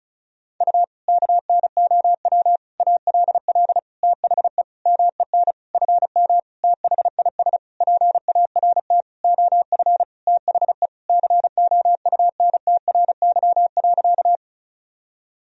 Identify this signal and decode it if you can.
Morse code, heard as a keyed tone, about 36 wpm, 700 Hz. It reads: U KNOW ALL THE MEN FM THIS PART OF THE COUNTRY.
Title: Sentences from Top 300 Words and Abbreviations 35wpm